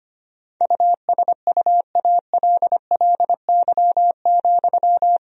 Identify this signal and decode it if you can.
Morse code, heard as a keyed tone, about 25 wpm, 700 Hz. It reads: USUALLY,
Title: Sentences from Top 1000 Words - Review Part 3 - 25wpm